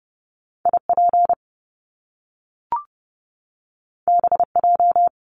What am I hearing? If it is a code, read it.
Morse code, heard as a keyed tone, about 30 wpm, 700 Hz. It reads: IP BJ
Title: Top Two Letter Combinations in Words 30wpm